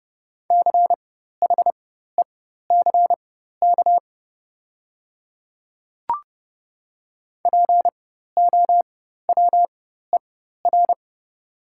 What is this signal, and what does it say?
Morse code, heard as a keyed tone, about 30 wpm, spaced as for 14 wpm, 690 Hz. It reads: CHECK POWER
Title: 4x Character Spacing - Top 500 Words - 30wpm